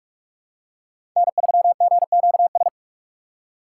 Morse code, heard as a keyed tone, about 41 wpm, 700 Hz. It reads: N3GQS
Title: USA Call Signs - 5 Characters - 3x Character Spacing - 45wpm